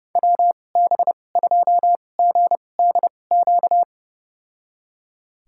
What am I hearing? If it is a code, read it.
Morse code, heard as a keyed tone, about 30 wpm, spaced as for 22 wpm, 700 Hz. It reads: WB2GDQ
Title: USA Call Signs - 6 Characters - 2x Character Spacing - 30wpm